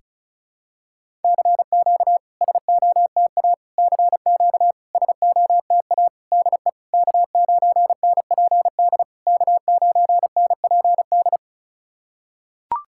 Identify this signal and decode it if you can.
Morse code, heard as a keyed tone, about 36 wpm, 690 Hz. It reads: CQ SOTA CQ SOTA DE K9NPD K9NPD
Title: SOTA QSO Practice 35wpm